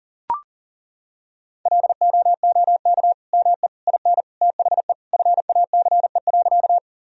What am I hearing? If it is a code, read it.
Morse code, heard as a keyed tone, about 37 wpm, 700 Hz. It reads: LOOK ME IN THE FACE.
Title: Sentences from Top 1000 Words - Review Part 2 - 40wpm